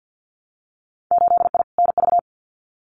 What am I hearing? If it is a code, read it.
Morse code, heard as a keyed tone, about 50 wpm, 700 Hz. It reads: 8I NV
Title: Winter Field Day Exchanges Encore 50wpm